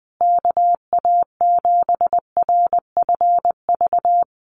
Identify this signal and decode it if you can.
Morse code, heard as a keyed tone, about 20 wpm, 700 Hz. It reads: KA7RFV
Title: USA Call Signs Repeated 2x 20wpm